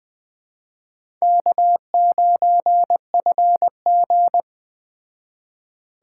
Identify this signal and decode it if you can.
Morse code, heard as a keyed tone, about 20 wpm, 700 Hz. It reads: K9FG
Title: Speed Racing - USA 1x2 Call Signs 17wpm